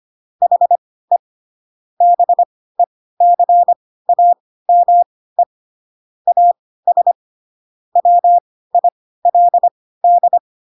Morse code, heard as a keyed tone, about 25 wpm, spaced as for 16 wpm, 700 Hz. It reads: HE BECAME AS WILD
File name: Pinocchio / Pinocchio ch2 10 25 ik1hge